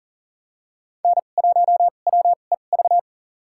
Morse code, heard as a keyed tone, about 40 wpm, spaced as for 29 wpm, 700 Hz. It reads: N1WEV
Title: USA Call Signs - 2x Character Spacing - 40wpm